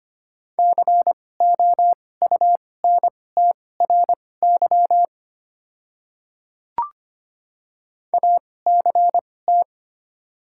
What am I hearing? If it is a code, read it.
Morse code, heard as a keyed tone, about 25 wpm, spaced as for 18 wpm, 690 Hz. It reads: COUNTRY ACT
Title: 2x Character Spacing - Top 300 Words - 25wpm